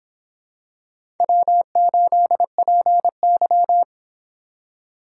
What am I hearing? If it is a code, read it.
Morse code, heard as a keyed tone, about 26 wpm, 700 Hz. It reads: W8PY